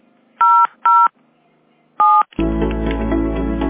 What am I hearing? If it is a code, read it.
DTMF, heard as telephone touch tones, tones of about 250 ms. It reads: 00 7